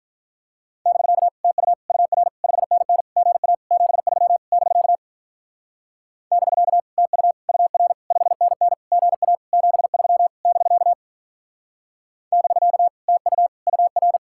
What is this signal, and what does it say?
Morse code, heard as a keyed tone, about 46 wpm, 700 Hz. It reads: <BK> TU UR 5NN GA 73 <BK> <BK> TU UR 5NN GA 73 <BK> <BK> TU UR